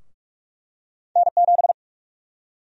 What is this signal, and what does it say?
Morse code, heard as a keyed tone, about 42 wpm, 700 Hz. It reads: N7
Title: USA Call Sign Prefixes - 45wpm